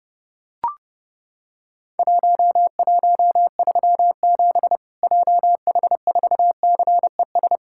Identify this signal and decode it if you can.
Morse code, heard as a keyed tone, about 30 wpm, 700 Hz. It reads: 1137 JH4CES
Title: Worked All Europe DX Contest - QTC Exchange - 30wpm